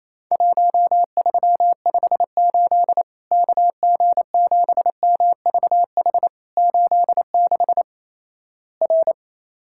Morse code, heard as a keyed tone, about 28 wpm, 700 Hz. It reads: 1358 KG7MVH 86 R
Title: Worked All Europe DX Contest - QTC Exchange - 28wpm